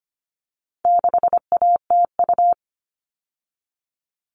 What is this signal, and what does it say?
Morse code, heard as a keyed tone, about 25 wpm, 700 Hz. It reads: 6ATU
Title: All Cut Numbers - 4-Digit 25wpm